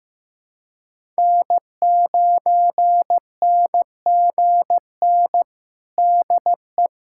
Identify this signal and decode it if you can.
Morse code, heard as a keyed tone, about 15 wpm, 700 Hz. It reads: N9NGN DE
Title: USA Callsign DE Callsign 15wpm